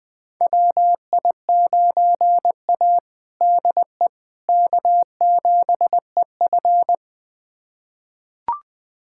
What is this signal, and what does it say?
Morse code, heard as a keyed tone, about 20 wpm, 700 Hz. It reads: WI9A DE K7EF
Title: USA Callsign DE Callsign 20wpm